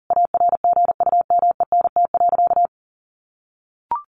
Morse code, heard as a keyed tone, about 40 wpm, 700 Hz. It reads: ARGUMENT.
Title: A Sound Of Thunder 40wpm